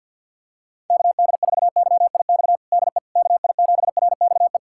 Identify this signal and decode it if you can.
Morse code, heard as a keyed tone, about 51 wpm, 700 Hz. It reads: KD4YIX DE KI7RXE
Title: USA Callsign DE Callsign - 6 Characters - 50wpm